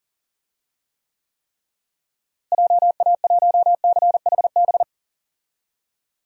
Morse code, heard as a keyed tone, about 40 wpm, 700 Hz. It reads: JA1CHB
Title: DXCC-100 Call Signs - 6 Characters - Repeated 3x - 40wpm